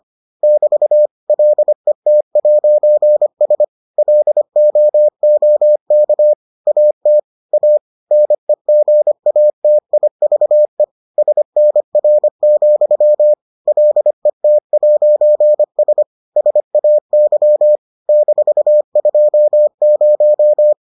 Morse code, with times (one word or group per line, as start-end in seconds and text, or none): 0.4-1.1 <BT>
1.3-3.6 LET'S
4.0-6.3 LOOK
6.7-7.2 AT
7.5-7.8 A
8.1-10.8 NEGATIVE
11.2-13.3 SNR,
13.7-16.0 LET'S
16.4-17.8 SAY
18.1-20.7 -20